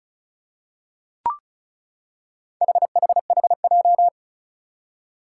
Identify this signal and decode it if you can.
Morse code, heard as a keyed tone, about 36 wpm, 700 Hz. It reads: HHHJ
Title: Four Challenging Letter-Number Combinations 35wpm